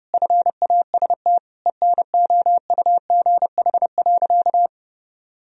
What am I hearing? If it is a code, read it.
Morse code, heard as a keyed tone, about 32 wpm, 700 Hz. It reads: FAST ENOUGH.
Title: Sentences from Top 1000 Words - Review Part 3 - 30wpm